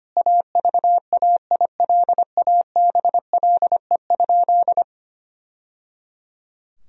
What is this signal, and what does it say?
Morse code, heard as a keyed tone, about 25 wpm, 700 Hz. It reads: AVAILABLE?
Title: Sentences from Top 1000 Words - Review Part 3 - 25wpm